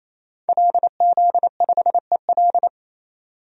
Morse code, heard as a keyed tone, about 28 wpm, 700 Hz. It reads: LZ5EL
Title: DXCC-100 Call Signs - 5 Characters - 28wpm